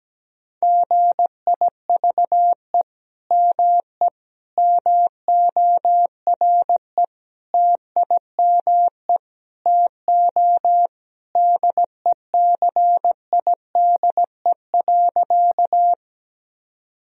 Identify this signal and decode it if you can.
Morse code, heard as a keyed tone, about 17 wpm, 700 Hz. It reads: GIVE ME MORE TIME TO DECIDE.
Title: Sentences from Top 1000 Words - Review Part 3 - 17wpm